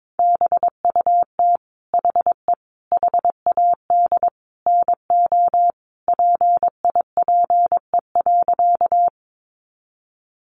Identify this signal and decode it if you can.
Morse code, heard as a keyed tone, about 22 wpm, 700 Hz. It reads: BUT HE HAD NO PIPE.
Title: Sentences from Top 1500 Words 22wpm